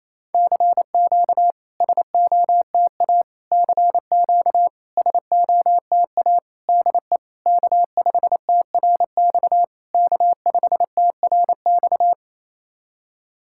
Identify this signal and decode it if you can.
Morse code, heard as a keyed tone, about 28 wpm, 700 Hz. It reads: CQ SOTA CQ SOTA DE K5TRX K5TRX
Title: SOTA QSO Practice 28wpm